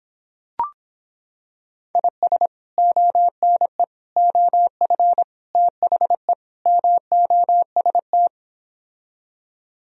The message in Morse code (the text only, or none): IS ONE OF THE MOST